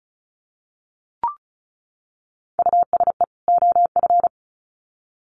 Morse code, heard as a keyed tone, about 34 wpm, 700 Hz. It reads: USE OF
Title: Sets of 2 Words 35wpm